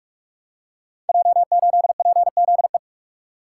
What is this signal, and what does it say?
Morse code, heard as a keyed tone, about 44 wpm, 690 Hz. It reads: J8PZE
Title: Code Groups - 5 Letter:Numbers - 45wpm